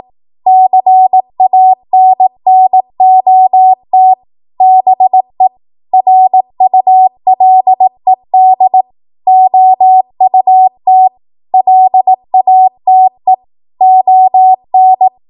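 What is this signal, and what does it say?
Morse code, heard as a keyed tone, about 18 wpm, 750 Hz. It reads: CANNOT BE RULED OUT LATE ON
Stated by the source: Code practice at 18 WPM, a transition run bridging 15 to 20 WPM.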